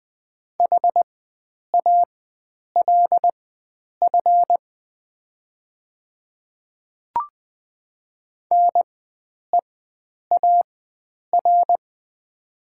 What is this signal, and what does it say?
Morse code, heard as a keyed tone, about 20 wpm, spaced as for 9 wpm, 700 Hz. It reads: HALF NEAR
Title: 4x Character Spacing - Top 500 Words - 20wpm